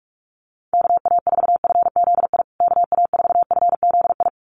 Morse code, heard as a keyed tone, about 44 wpm, 700 Hz. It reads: KA4FZI KA4FZI
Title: USA Call Signs Repeated 2x 45wpm